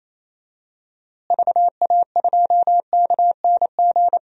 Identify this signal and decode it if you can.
Morse code, heard as a keyed tone, about 27 wpm, 700 Hz. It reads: VA2KNG
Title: DXCC-100 Call Signs - 6 Characters - Repeated 2x - 28wpm